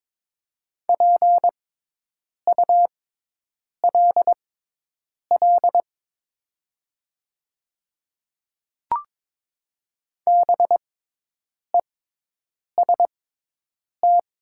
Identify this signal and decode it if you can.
Morse code, heard as a keyed tone, about 22 wpm, spaced as for 8 wpm, 700 Hz. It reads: PULL BEST
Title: 6x Character Spacing - Top 500 Words - 22wpm